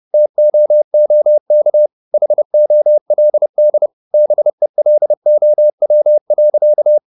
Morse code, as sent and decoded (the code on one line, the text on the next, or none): - --- --- -.- / .... --- .-.. -.. / -... .-.. --- .-- .-.-.-
TOOK HOLD BELOW.